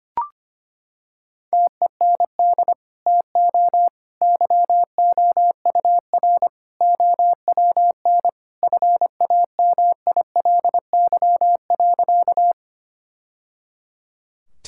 Morse code, heard as a keyed tone, about 25 wpm, 700 Hz. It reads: TEND TO YOUR OWN FAMILY.
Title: Sentences from Top 1000 Words - Review Part 5 - 25wpm